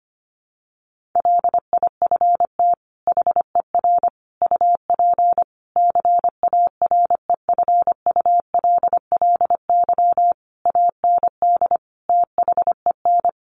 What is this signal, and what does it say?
Morse code, heard as a keyed tone, about 25 wpm, 700 Hz. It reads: LIFT HER UP CAREFULLY AND THEN
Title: Sentences from Top 1300 Words 25wpm